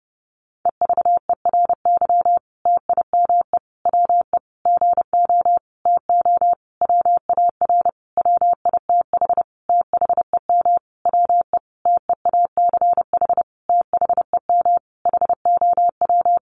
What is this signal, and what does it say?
Morse code, heard as a keyed tone, about 30 wpm, 700 Hz. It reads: EVERY TIME WE GO TO WAR WITH THEM WE TEACH THEM HOW
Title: Sentences from Top 1300 Words 30wpm